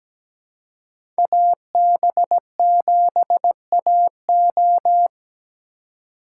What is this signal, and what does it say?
Morse code, heard as a keyed tone, about 17 wpm, 700 Hz. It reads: AB7AO
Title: USA Call Signs - 5 Characters - 3x Character Spacing - 17wpm